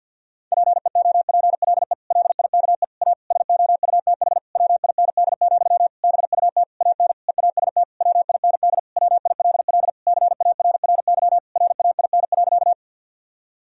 Morse code, heard as a keyed tone, about 50 wpm, 700 Hz. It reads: PEOPLE LIKE A SOUTH WIND, BUT AN EAST WIND WILL CARRY RAIN.
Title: Sentences from Top 1000 Words - Review Part 3 - 50wpm